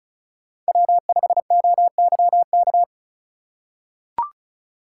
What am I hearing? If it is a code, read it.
Morse code, heard as a keyed tone, about 36 wpm, 700 Hz. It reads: W5OYK
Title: USA Call Signs - 4x Character Spacing - 35wpm